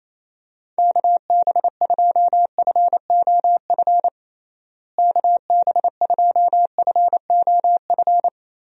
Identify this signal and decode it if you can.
Morse code, heard as a keyed tone, about 28 wpm, 700 Hz. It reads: KB2FOF KB2FOF
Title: USA Call Signs - 6 Characters - Repeated 2x - 28wpm